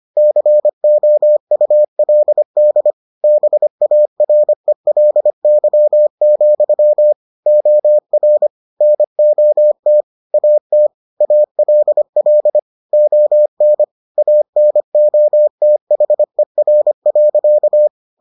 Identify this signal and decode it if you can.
Morse code, heard as a keyed tone, about 25 wpm, 600 Hz. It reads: COULD BARELY, OR NOT AT ALL ON ANOTHER.